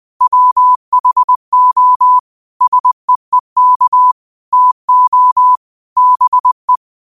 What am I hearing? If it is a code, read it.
Morse code, heard as a keyed tone, about 20 wpm, 1000 Hz. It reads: WHO SEEK TO BE
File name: qotdfiles / 20wpm